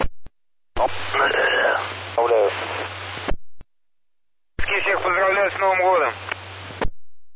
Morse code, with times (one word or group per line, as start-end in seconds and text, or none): none